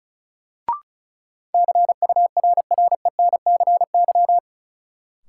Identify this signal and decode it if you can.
Morse code, heard as a keyed tone, about 36 wpm, 700 Hz. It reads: CURRENCY
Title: Sentences from Top 1500 Words - New Word Drill - 35wpm